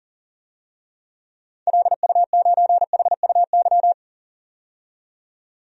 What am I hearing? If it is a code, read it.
Morse code, heard as a keyed tone, about 40 wpm, 700 Hz. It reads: LU9HUY
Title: DXCC-100 Call Signs - 6 Characters - 40wpm